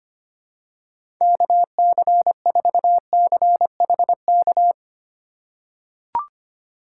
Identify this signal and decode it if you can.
Morse code, heard as a keyed tone, about 25 wpm, 700 Hz. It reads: KC4CHK
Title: USA Call Signs - 2x Character Spacing - 25wpm